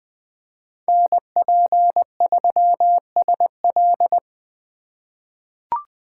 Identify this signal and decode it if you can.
Morse code, heard as a keyed tone, about 21 wpm, 700 Hz. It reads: NP3SL
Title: USA Call Signs - 5 Characters - 4x Character Spacing - 20wpm